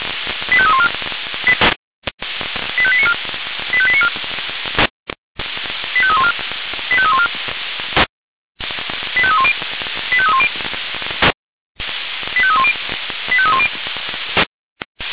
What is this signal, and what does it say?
Five-tone selective call (ZVEI), heard as quick runs of five tones, tones of about 70 ms. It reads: 85314 85894 85894 85314 85314 85310 85310 85310 85310
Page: SDR/ZVEI